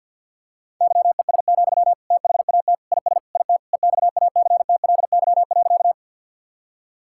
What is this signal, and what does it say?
Morse code, heard as a keyed tone, about 50 wpm, 700 Hz. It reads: YES, THAT IS IT EXACTLY.
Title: Sentences from Top 1000 Words - Review Part 6 - 50wpm